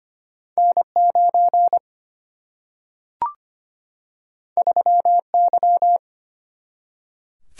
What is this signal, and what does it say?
Morse code, heard as a keyed tone, about 25 wpm, 700 Hz. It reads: N 9 3 Y
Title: Single-Letter Single-Number 25wpm